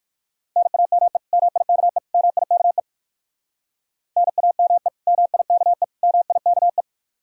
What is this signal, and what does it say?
Morse code, heard as a keyed tone, about 59 wpm, 700 Hz. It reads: NAME MIKE MIKE NAME MIKE MIKE